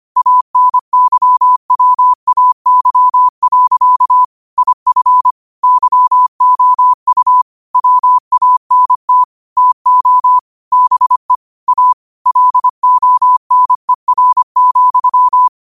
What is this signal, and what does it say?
Morse code, heard as a keyed tone, about 25 wpm, 1000 Hz. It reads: ANYWAY. IF YOU WANT TO BE A LONER,
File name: qotdfiles / 25wpm